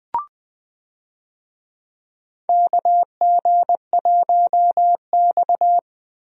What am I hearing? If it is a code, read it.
Morse code, heard as a keyed tone, about 20 wpm, 700 Hz. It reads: KG1X
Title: USA Call Signs - 4 Characters - Repeated 2x - 20wpm